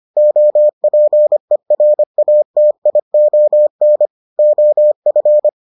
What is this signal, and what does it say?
Morse code, heard as a keyed tone, about 25 wpm, 600 Hz. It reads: OPERATION OF